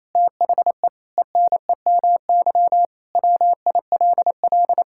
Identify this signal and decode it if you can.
Morse code, heard as a keyed tone, about 27 wpm, 700 Hz. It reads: THE ENEMY WILL
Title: Pangram Challenge v2 - 28wpm